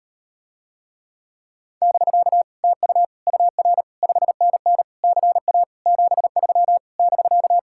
Morse code, heard as a keyed tone, about 39 wpm, 700 Hz. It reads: <BK> TU UR 5NN CA 73 <BK>